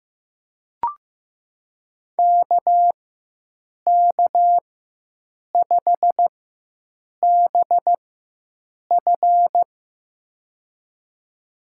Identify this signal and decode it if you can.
Morse code, heard as a keyed tone, about 16 wpm, spaced as for 7 wpm, 700 Hz. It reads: KK5BF
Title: USA Call Signs - 5 Characters - 4x Character Spacing - 15wpm